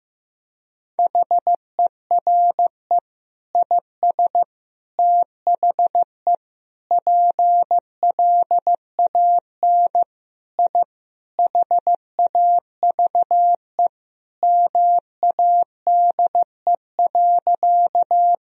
Morse code, heard as a keyed tone, about 16 wpm, 700 Hz. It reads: HERE IS THE PLAN I HAVE MADE.
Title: Sentences from Top 1000 Words - Review Part 2 - 15wpm